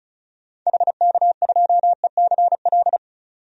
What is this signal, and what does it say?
Morse code, heard as a keyed tone, about 35 wpm, 700 Hz. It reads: HK2ECL
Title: DXCC-100 Call Signs - 6 Characters - 35wpm